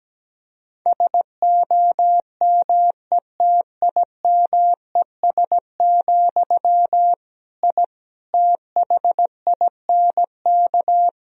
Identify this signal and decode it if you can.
Morse code, heard as a keyed tone, about 17 wpm, 700 Hz. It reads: SOMETIMES, I THINK
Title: Sentences from Top 1000 Words - Review Part 3 - 17wpm